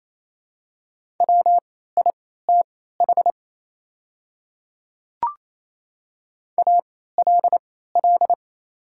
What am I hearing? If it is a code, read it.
Morse code, heard as a keyed tone, about 28 wpm, spaced as for 16 wpm, 700 Hz. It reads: WITH ALL
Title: 3x Character Spacing - Top 100 Words - 28wpm